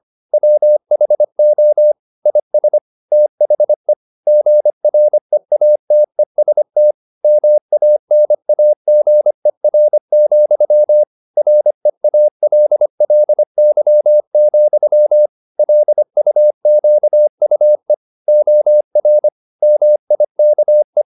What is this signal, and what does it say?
Morse code, heard as a keyed tone, about 25 wpm, 600 Hz. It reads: WHO IS THE GREATEST MANAGER, REALLY, LUQUE OR MIKE
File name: the old man and the sea part 1 25wpm